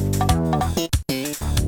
There is no code.